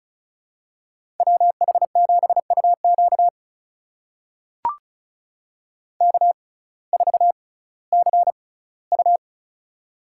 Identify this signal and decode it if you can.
Morse code, heard as a keyed tone, about 36 wpm, spaced as for 19 wpm, 700 Hz. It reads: WH7UQ K4CU
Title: USA Call Signs - 6x Character Spacing - 35wpm